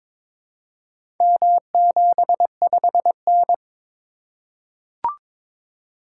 Morse code, heard as a keyed tone, about 22 wpm, 700 Hz. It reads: M75N
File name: practice / 22wpm / Four Letter-Number Combinations - 22wpm